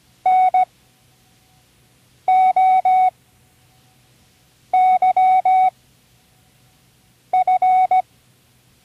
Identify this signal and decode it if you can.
Morse code, heard as a keyed tone, about 17 wpm, spaced as for 5 wpm, 750 Hz. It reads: NOYF
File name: CWResources / Learning Morse Code (from Tune in the World) / 24 random